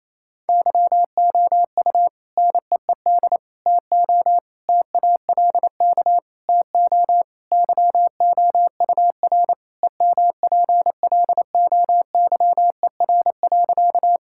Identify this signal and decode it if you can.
Morse code, heard as a keyed tone, about 28 wpm, 700 Hz. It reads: YOU NEED TO TALK TO YOUR EMPLOYER.
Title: Sentences from Top 1600 Words 28wpm